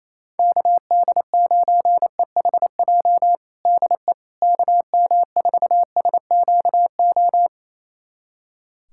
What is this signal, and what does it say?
Morse code, heard as a keyed tone, about 28 wpm, 700 Hz. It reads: KD9EHJ DE KM4SQO
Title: USA Callsign DE Callsign 28wpm